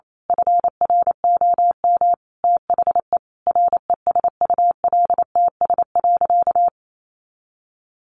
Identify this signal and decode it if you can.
Morse code, heard as a keyed tone, about 28 wpm, 700 Hz. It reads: FROM THE RESULTS.
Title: A Sound Of Thunder 28wpm